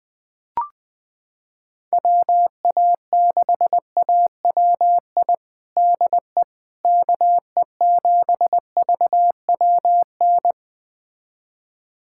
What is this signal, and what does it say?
Morse code, heard as a keyed tone, about 20 wpm, 700 Hz. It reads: WA6AWI DE KE7VWN
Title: USA Callsign DE Callsign - 6 Characters - 20wpm